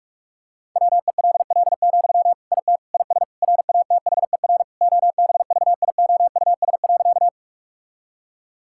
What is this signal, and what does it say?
Morse code, heard as a keyed tone, about 46 wpm, 700 Hz. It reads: WELL, IT IS RATHER OBVIOUS.
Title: Sentences from Top 1600 Words 45wpm